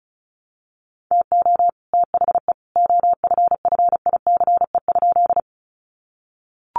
Morse code, heard as a keyed tone, about 34 wpm, 700 Hz. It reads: TO THE OFFICE?
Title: Sentences from Top 1000 Words 35wpm